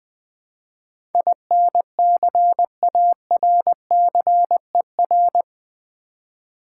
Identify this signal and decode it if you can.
Morse code, heard as a keyed tone, about 20 wpm, 700 Hz. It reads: INCARCER
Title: Build A Word - 4 Letters At A Time - 11-Letter Words - 20wpm